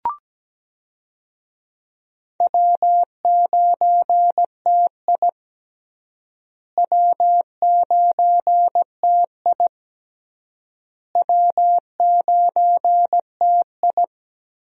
Morse code, heard as a keyed tone, about 17 wpm, 700 Hz. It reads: W9TI W9TI W9TI
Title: USA Call Signs - 4 Characters - Repeated 3x - 17wpm